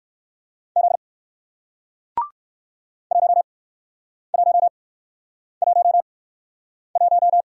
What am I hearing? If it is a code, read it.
Morse code, heard as a keyed tone, about 60 wpm, spaced as for 9 wpm, 700 Hz. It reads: Z 1111